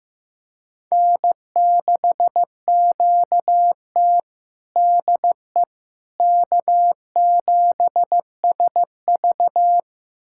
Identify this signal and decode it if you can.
Morse code, heard as a keyed tone, about 15 wpm, 700 Hz. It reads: N6QT DE K7SV
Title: USA Callsign DE Callsign - 4 Characters - 15wpm